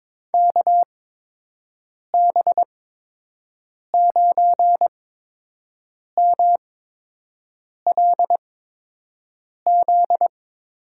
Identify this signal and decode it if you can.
Morse code, heard as a keyed tone, about 22 wpm, spaced as for 6 wpm, 700 Hz. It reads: KB9MLZ
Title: USA Call Signs - 6 Characters - 8x Character Spacing - 22wpm